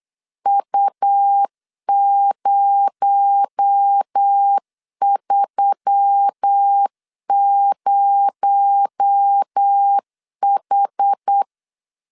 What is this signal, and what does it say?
Morse code, heard as a keyed tone, about 8 wpm, 800 Hz. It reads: U030H
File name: LEZIONI / 6 PKX0 / 6 PKX0 8wpm 3341